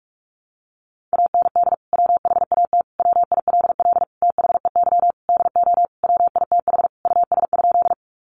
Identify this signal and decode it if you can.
Morse code, heard as a keyed tone, about 44 wpm, 700 Hz. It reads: AND WHAT WILL THEY DO WITH US?
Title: Sentences from Top 200 Words 45wpm